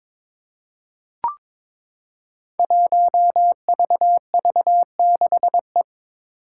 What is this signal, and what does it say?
Morse code, heard as a keyed tone, about 22 wpm, 700 Hz. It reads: 1VV6E
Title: Five Challenging Letter-Number Combinations 22wpm